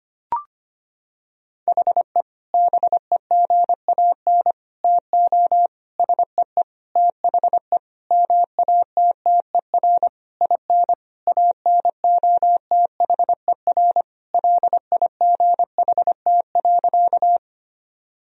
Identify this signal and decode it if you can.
Morse code, heard as a keyed tone, about 25 wpm, 700 Hz. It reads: HE BEGAN TO SEE THE MATTER IN ANOTHER LIGHT.
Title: Sentences from Top 1000 Words - Review Part 1 - 25wpm